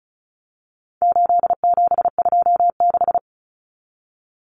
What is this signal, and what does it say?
Morse code, heard as a keyed tone, about 34 wpm, 700 Hz. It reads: 8726
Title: Common Cut Numbers - 4-Digit 35wpm